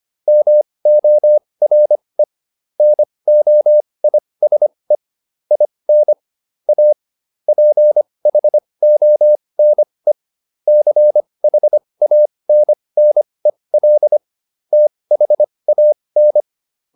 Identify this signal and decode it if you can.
Morse code, heard as a keyed tone, about 25 wpm, spaced as for 20 wpm, 600 Hz. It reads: MORE NOISE IN A PHONE CHANNEL THAN